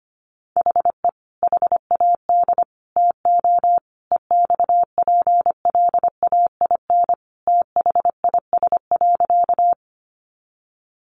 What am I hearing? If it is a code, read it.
Morse code, heard as a keyed tone, about 25 wpm, 700 Hz. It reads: HE HAD TO EXPLAIN THIS.
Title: Sentences from Top 1100 Words 25wpm